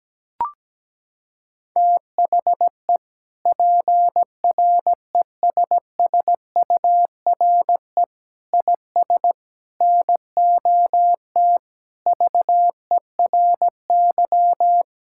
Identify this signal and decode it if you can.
Morse code, heard as a keyed tone, about 17 wpm, 700 Hz. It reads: THE PRESSURE IS NOT VERY
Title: Sentences from Top 1000 Words - Review Part 6 - 17wpm